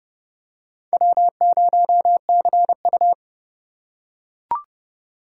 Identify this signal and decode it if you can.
Morse code, heard as a keyed tone, about 30 wpm, 700 Hz. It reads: W0CU
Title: USA Call Signs - 4 Characters - 2x Character Spacing - 30wpm